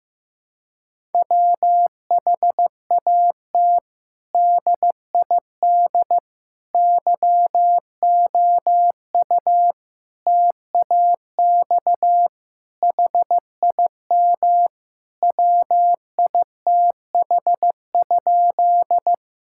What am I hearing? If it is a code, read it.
Morse code, heard as a keyed tone, about 15 wpm, 700 Hz. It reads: WHAT DID YOU TAX HIM WITH?
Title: Sentences from Top 1000 Words - Review Part 5 - 15wpm